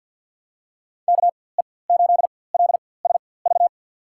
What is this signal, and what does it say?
Morse code, heard as a keyed tone, about 50 wpm, spaced as for 23 wpm, 700 Hz. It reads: KE8LSV
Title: USA Call Signs - 6 Characters - 4x Character Spacing - 50wpm